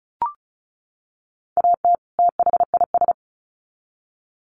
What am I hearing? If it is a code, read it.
Morse code, heard as a keyed tone, about 34 wpm, 700 Hz. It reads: AT THIS
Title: Sets of 2 Words 35wpm